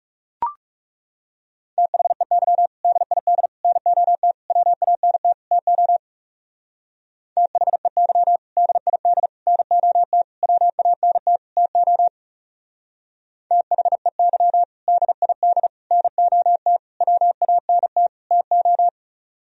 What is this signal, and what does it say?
Morse code, heard as a keyed tone, about 36 wpm, 700 Hz. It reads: THEY DID NOT WANT TO THEY DID NOT WANT TO THEY DID NOT WANT TO
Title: Speed-Racing - Sets of 5 Words 30wpm